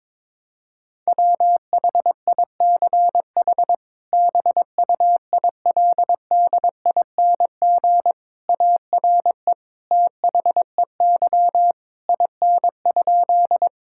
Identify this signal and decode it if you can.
Morse code, heard as a keyed tone, about 21 wpm, 700 Hz. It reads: WHICH BUILDING ARE THEY IN?
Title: Sentences from Top 1000 Words - Review Part 5 - 22wpm